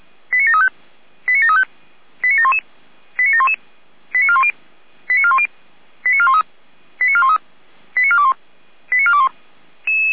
Five-tone selective call (ZVEI), heard as quick runs of five tones, tones of about 70 ms. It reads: 89735 89735 89711 89711 89410 89410 89413 89413 89421 89421